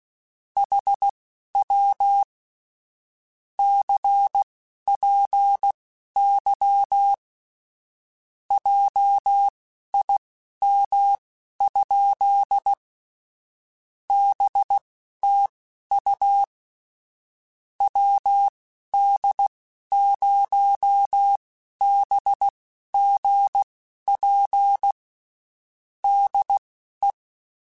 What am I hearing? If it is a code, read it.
Morse code, heard as a keyed tone, about 16 wpm, spaced as for 12 wpm, 800 Hz. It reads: HW CPY JIM? BTU WD0BGP DE
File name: QSOs / QSOs-mp3s / WD0BGP 1@15